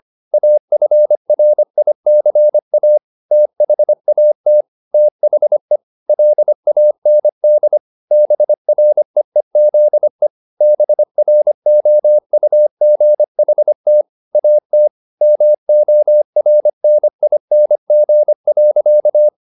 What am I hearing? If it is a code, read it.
Morse code, heard as a keyed tone, about 25 wpm, 600 Hz. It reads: AFRICA THAT THE LAND BREEZE BROUGHT AT MORNING.
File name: the old man and the sea part 1 25wpm